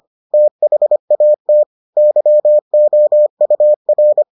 Morse code, heard as a keyed tone, about 25 wpm, 600 Hz. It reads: THAT YOUR